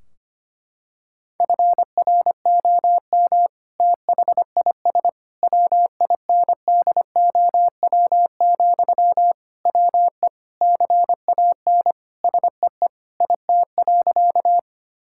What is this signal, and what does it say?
Morse code, heard as a keyed tone, about 25 wpm, 700 Hz. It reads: FROM THIS WINDOW, WE CAN SEE IT.
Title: Sentences from Top 1600 Words 25wpm